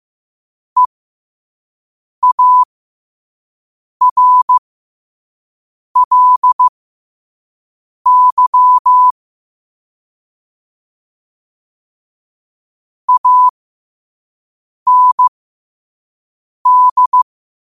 Morse code, heard as a keyed tone, about 15 wpm, spaced as for 5 wpm, 1000 Hz. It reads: EARLY AND
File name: qotdfiles / 5wpm